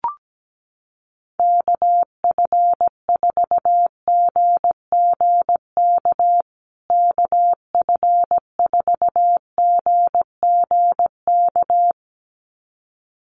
Morse code, heard as a keyed tone, about 17 wpm, 700 Hz. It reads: KF4GGK KF4GGK
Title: USA Call Signs Repeated 2x 17wpm